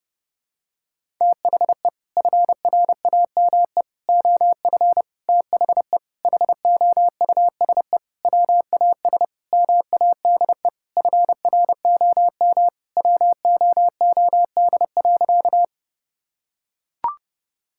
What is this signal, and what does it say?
Morse code, heard as a keyed tone, about 30 wpm, 700 Hz. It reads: THE FRAME OF THE HOUSE WAS MADE FROM WOOD.
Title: Sentences from Top 1000 Words - Review Part 6 - 30wpm